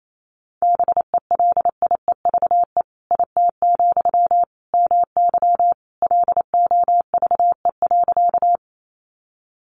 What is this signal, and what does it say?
Morse code, heard as a keyed tone, about 28 wpm, 700 Hz. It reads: BELIEVE IT, MY LOVE.
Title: Sentences from Top 1000 Words 28wpm